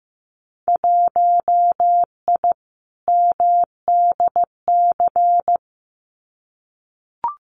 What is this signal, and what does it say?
Morse code, heard as a keyed tone, about 15 wpm, 700 Hz. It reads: 1I MDC
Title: Winter Field Day Exchanges 15wpm